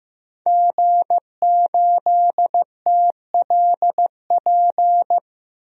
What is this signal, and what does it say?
Morse code, heard as a keyed tone, about 16 wpm, 700 Hz. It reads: G8TLP
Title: DXCC-100 Call Signs - 5 Characters - 15wpm